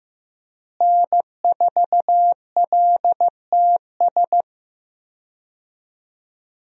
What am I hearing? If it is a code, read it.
Morse code, heard as a keyed tone, about 15 wpm, 700 Hz. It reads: N4LTS
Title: USA Call Signs - 5 Characters - Repeated 2x - 15wpm